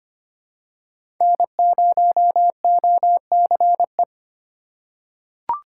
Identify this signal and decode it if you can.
Morse code, heard as a keyed tone, about 25 wpm, 700 Hz. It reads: N0OCE
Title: USA Call Signs - 6x Character Spacing - 25wpm